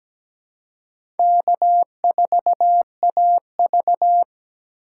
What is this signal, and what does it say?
Morse code, heard as a keyed tone, about 17 wpm, 700 Hz. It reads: K4AV
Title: USA Call Signs - 4 Characters - 2x Character Spacing - 17wpm